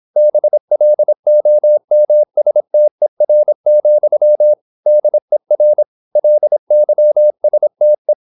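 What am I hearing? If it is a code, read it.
Morse code, heard as a keyed tone, about 26 wpm, 600 Hz. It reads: BLOMSTER, DER LYSTE
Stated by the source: High-speed (QRQ) Morse at 26 wpm.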